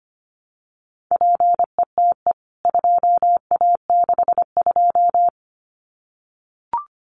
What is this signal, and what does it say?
Morse code, heard as a keyed tone, about 25 wpm, 700 Hz. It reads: PETE 2A62
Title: CWT Exchange 25wpm